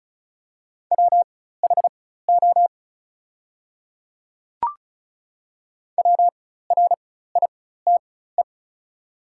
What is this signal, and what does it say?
Morse code, heard as a keyed tone, about 36 wpm, spaced as for 16 wpm, 700 Hz. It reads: WHO WRITE
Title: 4x Character Spacing - Top 200 Words - 35wpm